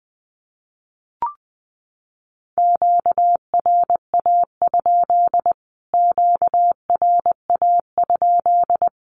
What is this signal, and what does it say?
Morse code, heard as a keyed tone, about 20 wpm, 700 Hz. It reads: QRA? QRA?
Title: Q-Codes Repeated 3x 20wpm